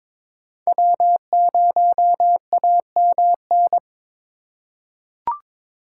Morse code, heard as a keyed tone, about 22 wpm, 700 Hz. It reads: W0AMN
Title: USA Call Signs - 5 Characters - 4x Character Spacing - 22wpm